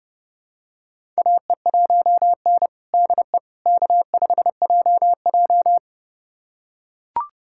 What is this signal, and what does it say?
Morse code, heard as a keyed tone, about 32 wpm, 700 Hz. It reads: AE1N DE K5JJ
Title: USA Callsign DE Callsign - 4 Characters - 30wpm